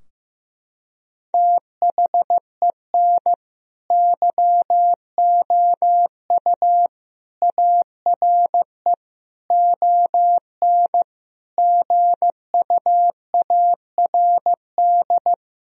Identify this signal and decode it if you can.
Morse code, heard as a keyed tone, about 15 wpm, 700 Hz. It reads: THEN YOU ARE ON GUARD
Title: Sentences from Top 1000 Words - Review Part 4 - 15wpm